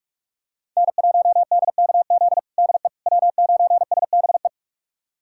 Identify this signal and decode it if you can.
Morse code, heard as a keyed tone, about 46 wpm, 700 Hz. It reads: N1DKZ DE W9SBE
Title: USA Callsign DE Callsign - 5 Characters - 45wpm